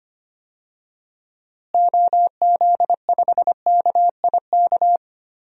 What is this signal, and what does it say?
Morse code, heard as a keyed tone, about 25 wpm, 700 Hz. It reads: OZ5KIK
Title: DXCC-100 Call Signs - 6 Characters - Repeated 2x - 25wpm